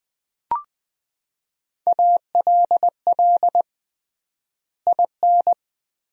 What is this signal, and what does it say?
Morse code, heard as a keyed tone, about 21 wpm, 700 Hz. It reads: ALL IN
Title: 3x Word-Spacing - Sentences from Top 200 Words - 20wpm